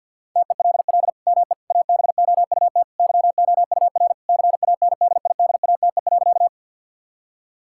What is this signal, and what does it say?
Morse code, heard as a keyed tone, about 51 wpm, 700 Hz. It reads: TELL ME ABOUT YOUR CANDIDATE.
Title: Sentences from Top 1600 Words 50wpm